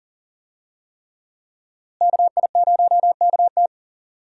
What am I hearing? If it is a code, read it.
Morse code, heard as a keyed tone, about 38 wpm, 700 Hz. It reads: KI0KT
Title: USA Call Signs - 5 Characters - Repeated 3x - 40wpm